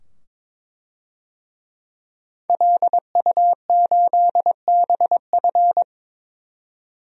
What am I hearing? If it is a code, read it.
Morse code, heard as a keyed tone, about 22 wpm, 700 Hz. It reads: LU8BF
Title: DXCC-100 Call Signs - 5 Characters - 22wpm